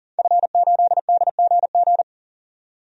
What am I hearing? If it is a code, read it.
Morse code, heard as a keyed tone, about 40 wpm, 700 Hz. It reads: F8DGG
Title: DXCC-100 Call Signs - 5 Characters - 40wpm